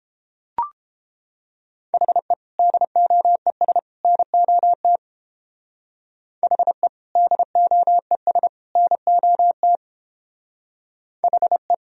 Morse code, heard as a keyed tone, about 30 wpm, 700 Hz. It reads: HE DOES NOT HE DOES NOT HE